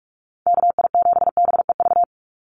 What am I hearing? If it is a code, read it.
Morse code, heard as a keyed tone, about 46 wpm, 700 Hz. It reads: KI7BEV
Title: USA Call Signs 45wpm